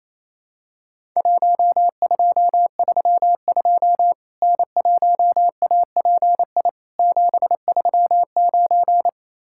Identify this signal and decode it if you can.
Morse code, heard as a keyed tone, about 28 wpm, 700 Hz. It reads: 1232 N1API 739
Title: Worked All Europe DX Contest - QTC Exchange - 28wpm